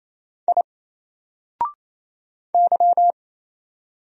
Morse code, heard as a keyed tone, about 27 wpm, 700 Hz. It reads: I Y
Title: Single Character (Letter:Number:Punctuation:Prosign) Alt 28wpm